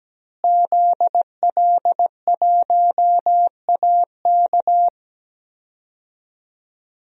Morse code, heard as a keyed tone, about 17 wpm, 700 Hz. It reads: ZL1AK
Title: DXCC-100 Call Signs - 5 Characters - Repeated 3x - 17wpm